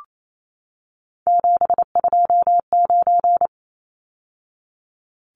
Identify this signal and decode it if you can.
Morse code, heard as a keyed tone, about 28 wpm, 700 Hz. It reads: 729
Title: Numbers - 3-Digit 28wpm